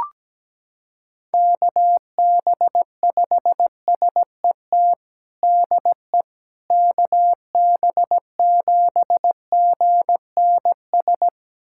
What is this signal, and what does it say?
Morse code, heard as a keyed tone, about 17 wpm, 700 Hz. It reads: KB5SET DE KB7GNS
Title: USA Callsign DE Callsign - 6 Characters - 17wpm